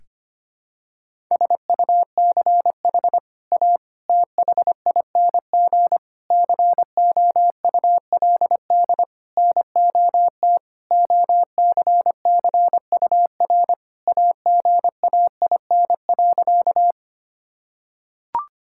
Morse code, heard as a keyed tone, about 25 wpm, 700 Hz. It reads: SUCH A THING COULD NOT OCCUR AGAIN.
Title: Sentences from Top 1000 Words - Review Part 6 - 25wpm